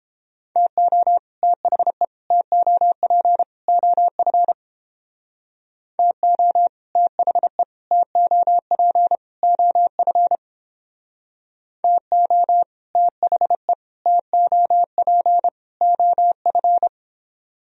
Morse code, heard as a keyed tone, about 31 wpm, 700 Hz. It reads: TO THE TOP OF TO THE TOP OF TO THE TOP OF